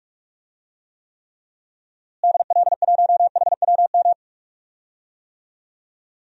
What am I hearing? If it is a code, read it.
Morse code, heard as a keyed tone, about 43 wpm, 700 Hz. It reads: DL1HWM
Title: DXCC-100 Call Signs - 6 Characters - Repeated 3x - 45wpm